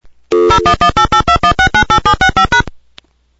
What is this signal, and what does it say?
DTMF, heard as telephone touch tones, tones of about 90 ms. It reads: 0118821698765#